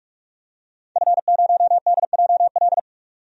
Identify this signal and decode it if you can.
Morse code, heard as a keyed tone, about 48 wpm, 700 Hz. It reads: F0DJL
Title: DXCC-100 Call Signs - 5 Characters - 45wpm